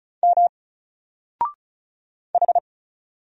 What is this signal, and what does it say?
Morse code, heard as a keyed tone, about 35 wpm, 700 Hz. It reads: M H
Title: Single Character (Letter:Number:Punctuation:Prosign) 35wpm